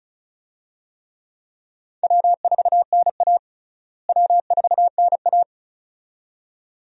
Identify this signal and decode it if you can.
Morse code, heard as a keyed tone, about 36 wpm, 700 Hz. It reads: W4NA W4NA
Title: USA Call Signs - 4 Characters - Repeated 2x - 35wpm